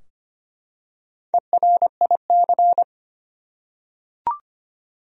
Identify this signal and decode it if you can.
Morse code, heard as a keyed tone, about 25 wpm, 700 Hz. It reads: ERIC
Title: USA Common Names - Max 5 Letters - 25wpm